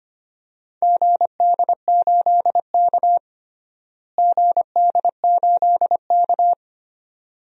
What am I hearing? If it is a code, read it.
Morse code, heard as a keyed tone, about 25 wpm, 700 Hz. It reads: GD8K GD8K
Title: DXCC-100 Call Signs - 4 Characters - Repeated 2x - 25wpm